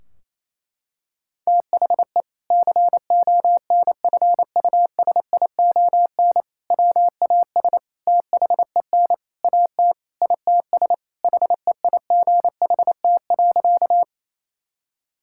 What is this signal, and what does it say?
Morse code, heard as a keyed tone, about 28 wpm, 700 Hz. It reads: THE CONFUSION WAS THEN AT ITS HEIGHT.
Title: Sentences from Top 1600 Words 28wpm